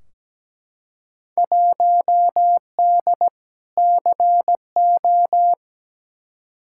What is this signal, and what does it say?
Morse code, heard as a keyed tone, about 17 wpm, 700 Hz. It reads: 1D CO
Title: Field Day Exchanges - 17wpm